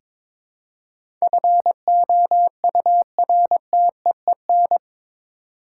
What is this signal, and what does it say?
Morse code, heard as a keyed tone, about 22 wpm, 700 Hz. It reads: FOURTEEN
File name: practice / 22wpm / Eight Letter Words - 22wpm